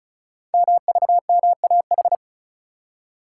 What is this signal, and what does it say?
Morse code, heard as a keyed tone, about 36 wpm, 690 Hz. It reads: MVMAH
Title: Code Groups - 5 Letters - 35wpm